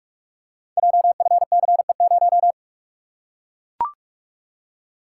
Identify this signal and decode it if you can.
Morse code, heard as a keyed tone, about 44 wpm, 690 Hz. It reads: JFCE0
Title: Code Groups - 5 Letter:Numbers - 45wpm